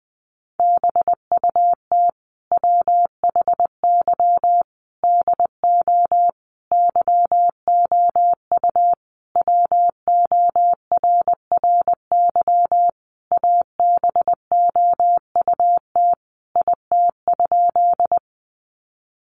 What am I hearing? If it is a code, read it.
Morse code, heard as a keyed tone, about 20 wpm, 700 Hz. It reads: BUT WHY DO YOU WORRY ABOUT IT?
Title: Sentences from Top 1100 Words 20wpm